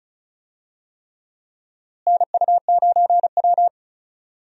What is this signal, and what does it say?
Morse code, heard as a keyed tone, about 36 wpm, 700 Hz. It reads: NU9W
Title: USA Call Signs - 4 Characters - Repeated 3x - 35wpm